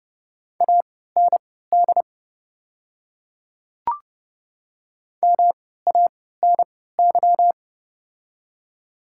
Morse code, heard as a keyed tone, about 30 wpm, spaced as for 17 wpm, 700 Hz. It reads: AND MANY
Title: 3x Character Spacing - Top 100 Words - 30wpm